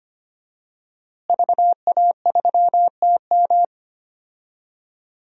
Morse code, heard as a keyed tone, about 25 wpm, 700 Hz. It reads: VA3TM
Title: DXCC-100 Call Signs - 5 Characters - Repeated 2x - 25wpm